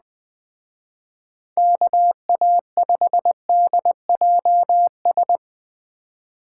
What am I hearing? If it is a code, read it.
Morse code, heard as a keyed tone, about 20 wpm, 700 Hz. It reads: KA5DJS
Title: Speed Racing - USA 2x3 Call Signs 17wpm